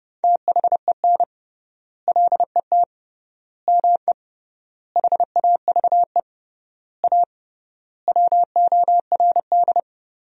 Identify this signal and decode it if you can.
Morse code, heard as a keyed tone, about 32 wpm, 700 Hz. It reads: THEN LET ME HAVE A WORD
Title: 3x Word-Spacing - Sentences from Top 300 Words - 30wpm